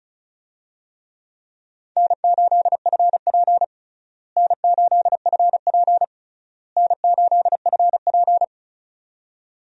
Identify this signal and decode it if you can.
Morse code, heard as a keyed tone, about 34 wpm, 700 Hz. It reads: N8FP N8FP N8FP
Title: USA Call Signs - 4 Characters - Repeated 3x - 35wpm